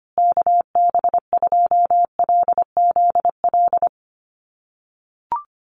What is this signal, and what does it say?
Morse code, heard as a keyed tone, about 25 wpm, 700 Hz. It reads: KB2LZL
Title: USA Call Signs Repeated 3x 25wpm